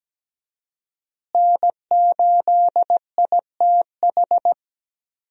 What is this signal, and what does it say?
Morse code, heard as a keyed tone, about 17 wpm, 700 Hz. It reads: N8ITH
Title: USA Call Signs - 5 Characters - Repeated 3x - 17wpm